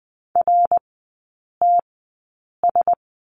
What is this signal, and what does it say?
Morse code, heard as a keyed tone, about 20 wpm, spaced as for 8 wpm, 700 Hz. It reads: RTS
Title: 053 Lesson 10e - Letter r - Three Characters 20wpm